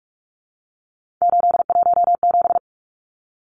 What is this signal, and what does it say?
Morse code, heard as a keyed tone, about 44 wpm, 700 Hz. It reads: 817
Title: Numbers - 3-Digit 45wpm